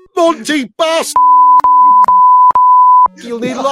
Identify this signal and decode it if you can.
Speech. There is no code.